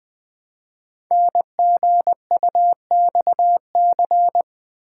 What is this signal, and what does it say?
Morse code, heard as a keyed tone, about 20 wpm, 700 Hz. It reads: NGUXC
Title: Code Groups - 5 Letter:Numbers - 20wpm